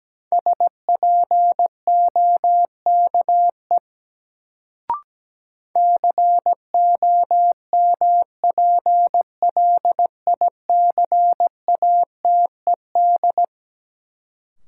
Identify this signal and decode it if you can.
Morse code, heard as a keyed tone, about 17 wpm, 700 Hz. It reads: SPOKE COMPLICATED
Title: Sentences from Top 1200 Words - New Word Drill Alt - 17wpm